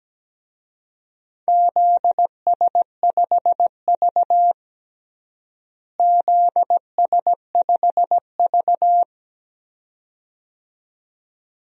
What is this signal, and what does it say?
Morse code, heard as a keyed tone, about 17 wpm, 700 Hz. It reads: ZS5V ZS5V
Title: DXCC-100 Call Signs - 4 Characters - Repeated 2x - 17wpm